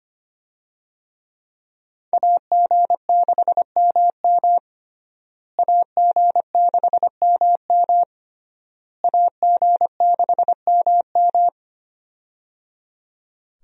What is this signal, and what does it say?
Morse code, heard as a keyed tone, about 25 wpm, 700 Hz. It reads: AG6MM AG6MM AG6MM
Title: USA Call Signs - 5 Characters - Repeated 3x - 25wpm